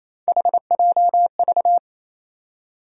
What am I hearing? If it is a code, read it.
Morse code, heard as a keyed tone, about 28 wpm, 700 Hz. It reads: HJV
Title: USA Call Sign Postfixes - 3 Letters 28wpm